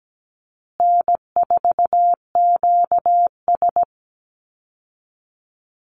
Morse code, heard as a keyed tone, about 17 wpm, 700 Hz. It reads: N4QS
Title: Easy Call Signs Repeated 2x - 17wpm